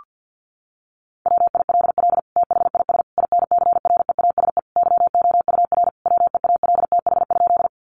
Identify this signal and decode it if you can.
Morse code, heard as a keyed tone, about 50 wpm, 700 Hz. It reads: WILL THIS INCREASE YOUR WEALTH?
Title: Sentences from Top 1200 Words 50wpm